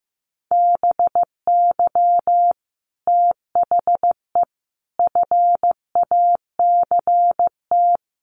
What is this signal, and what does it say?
Morse code, heard as a keyed tone, about 15 wpm, 700 Hz. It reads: BY THE FACT
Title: Sets of 4 Words 15wpm